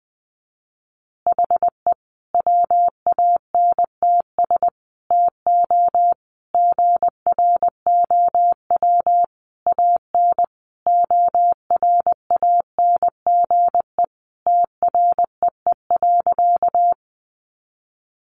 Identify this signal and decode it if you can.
Morse code, heard as a keyed tone, about 20 wpm, 700 Hz. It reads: HE WANTS TO GROW AN ORANGE TREE.
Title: Easy Phrases 20wpm